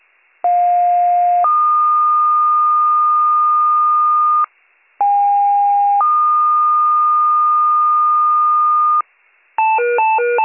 Two-tone page, heard as a one-second tone, then a three-second tone, about 710 then 1220 Hz, about 810 then 1220 Hz. It is heard from 0.4 to 9.0 s.